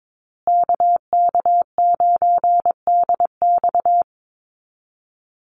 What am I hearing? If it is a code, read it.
Morse code, heard as a keyed tone, about 22 wpm, 700 Hz. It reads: KK9DX
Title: Call Signs Repeated 2x - 22wpm